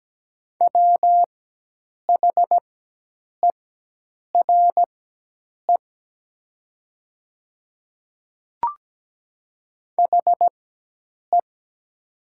Morse code, heard as a keyed tone, about 17 wpm, spaced as for 8 wpm, 700 Hz. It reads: WHERE HE
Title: 4x Character Spacing - Top 300 Words - 17wpm